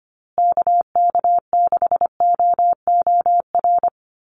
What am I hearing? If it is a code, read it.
Morse code, heard as a keyed tone, about 25 wpm, 700 Hz. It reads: KK6OOR
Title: USA Call Signs Repeated 3x 25wpm